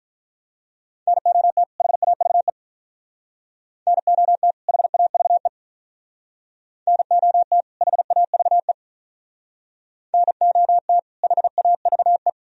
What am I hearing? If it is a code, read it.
Morse code, heard as a keyed tone, about 43 wpm, 700 Hz. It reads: NOT HAVE NOT HAVE NOT HAVE NOT HAVE
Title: Speed-Racing - Sets of 2 Words 35wpm